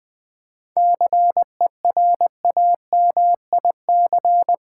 Morse code, heard as a keyed tone, about 21 wpm, 700 Hz. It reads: CERAMIC
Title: Difficult Words - Speed-Racing with Hints - 20wpm